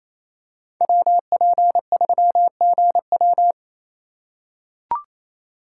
Morse code, heard as a keyed tone, about 28 wpm, 700 Hz. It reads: WP3GW
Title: USA Call Signs - 5 Characters - 6x Character Spacing - 28wpm